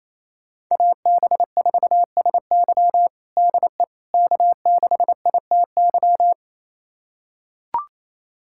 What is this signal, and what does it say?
Morse code, heard as a keyed tone, about 28 wpm, 700 Hz. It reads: AB4SY DE K6ITY
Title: USA Callsign DE Callsign - 5 Characters - 28wpm